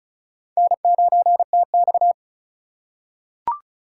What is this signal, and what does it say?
Morse code, heard as a keyed tone, about 34 wpm, 700 Hz. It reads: N9TX
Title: USA Call Signs - 4 Characters - 3x Character Spacing - 35wpm